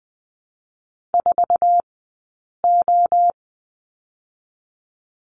Morse code, heard as a keyed tone, about 20 wpm, spaced as for 8 wpm, 700 Hz. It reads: 4O
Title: 080 Lesson 14d - Number 2 - Two Characters 20wpm